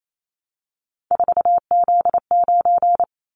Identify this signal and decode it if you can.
Morse code, heard as a keyed tone, about 28 wpm, 700 Hz. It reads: 4Z9
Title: Callsign Prefixes for Top 50 DXCC - 28wpm